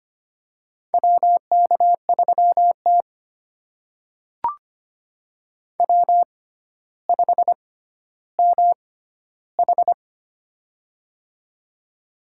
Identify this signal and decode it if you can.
Morse code, heard as a keyed tone, about 25 wpm, spaced as for 13 wpm, 700 Hz. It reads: WK3T W5MH
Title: USA Call Signs - 4 Characters - 6x Character Spacing - 25wpm